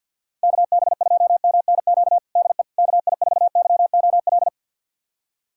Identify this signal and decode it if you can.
Morse code, heard as a keyed tone, about 53 wpm, 700 Hz. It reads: KB2MNQ DE KI4YOL
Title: USA Callsign DE Callsign - 6 Characters - 50wpm